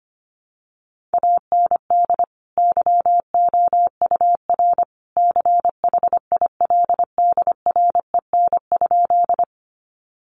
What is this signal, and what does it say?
Morse code, heard as a keyed tone, about 25 wpm, 700 Hz. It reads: AND YOUR CHILDREN?
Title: Sentences from Top 300 Words 25wpm